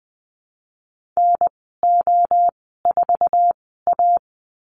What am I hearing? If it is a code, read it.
Morse code, heard as a keyed tone, about 20 wpm, spaced as for 14 wpm, 700 Hz. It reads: NO4A
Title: 055 Lesson 10g - Letter r - Call Signs 20wpm